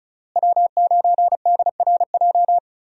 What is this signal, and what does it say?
Morse code, heard as a keyed tone, about 34 wpm, 700 Hz. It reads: W9DRJ
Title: USA Call Signs - 5 Characters - Repeated 3x - 35wpm